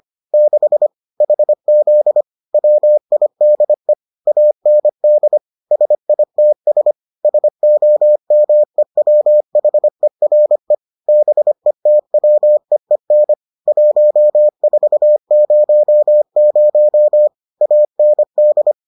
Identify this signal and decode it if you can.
Morse code, heard as a keyed tone, about 25 wpm, 600 Hz. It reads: 6 HZ WIDE AND SITS SOMEWHERE BETWEEN 1400 AND